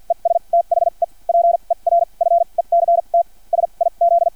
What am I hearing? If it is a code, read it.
Morse code, heard as a keyed tone, about 50 wpm, spaced as for 37 wpm, 680 Hz. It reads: ES THE PERFECT SIZ
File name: archived / CRSnet